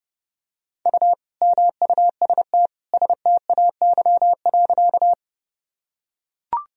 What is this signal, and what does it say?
Morse code, heard as a keyed tone, about 30 wpm, 700 Hz. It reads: U MUST STAY.
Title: Sentences from Top 500 Words and Abbreviations 30wpm